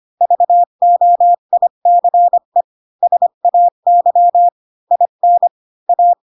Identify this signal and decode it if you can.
Morse code, heard as a keyed tone, about 25 wpm, 700 Hz. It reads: VOICE SAY IN A
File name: Pinocchio / Pinocchio ch1 20 25 ik1hge